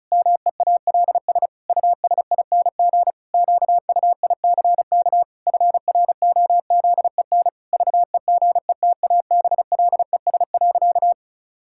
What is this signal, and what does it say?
Morse code, heard as a keyed tone, about 36 wpm, 700 Hz. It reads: MEALS USING QUICK FROZEN VEGETABLES.
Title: Pangram Challenge v2 - 35wpm